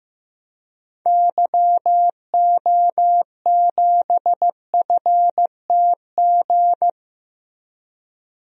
Morse code, heard as a keyed tone, about 15 wpm, 700 Hz. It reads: YO7FTG
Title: DXCC-100 Call Signs - 6 Characters - Repeated 3x - 15wpm